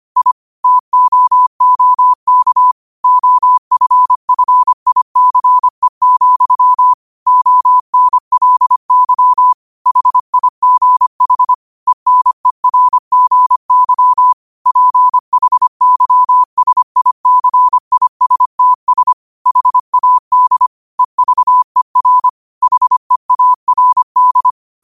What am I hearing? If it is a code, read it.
Morse code, heard as a keyed tone, about 25 wpm, 1000 Hz. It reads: I TOOK OFFICE, ONLY HIGH ENERGY PHYSICISTS HAD EVER HEARD
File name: qotdfiles / 25wpm